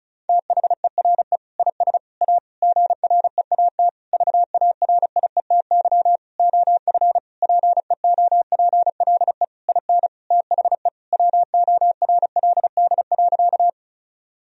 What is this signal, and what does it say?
Morse code, heard as a keyed tone, about 36 wpm, 700 Hz. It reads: THERE IS A GREAT VARIETY OF PEOPLE IN THE WORLD.
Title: Sentences from Top 1000 Words - Review Part 4 - 35wpm